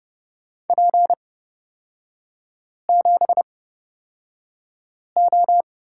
Morse code, heard as a keyed tone, about 30 wpm, spaced as for 5 wpm, 700 Hz. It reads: P7O
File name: Lesson 36e - BT - Three Characters 30wpm